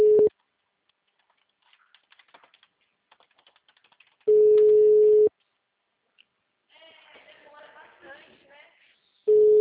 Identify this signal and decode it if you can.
Ringback tone (Europe, 425 Hz), with speech.
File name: out- -2004